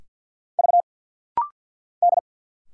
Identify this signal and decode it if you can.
Morse code, heard as a keyed tone, about 50 wpm, 700 Hz. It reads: V D
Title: Single Letters - Rapid-Fire 50wpm